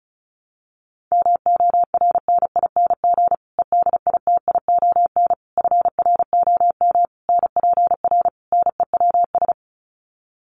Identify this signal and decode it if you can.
Morse code, heard as a keyed tone, about 34 wpm, 700 Hz. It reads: MORNING EDITION FROM NPR NEWS
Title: Sets of 5 Words Encore 35wpm